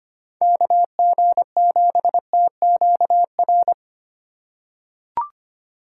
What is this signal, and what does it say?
Morse code, heard as a keyed tone, about 25 wpm, 700 Hz. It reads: KG7TQR
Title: USA Call Signs - 6 Characters - 3x Character Spacing - 25wpm